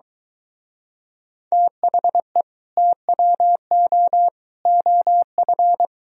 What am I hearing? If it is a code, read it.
Morse code, heard as a keyed tone, about 23 wpm, 700 Hz. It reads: THE TWO OF